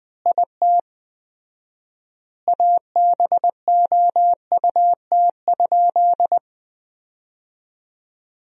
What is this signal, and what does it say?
Morse code, heard as a keyed tone, about 20 wpm, 700 Hz. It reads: IT ABOUT?
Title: 4x Word-Spacing - Sentences from Top 100 Words - 20wpm